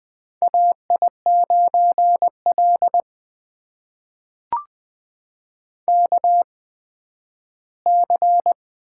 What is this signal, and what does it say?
Morse code, heard as a keyed tone, about 20 wpm, spaced as for 12 wpm, 700 Hz. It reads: AI9L KC
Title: USA Call Signs - 8x Character Spacing - 20wpm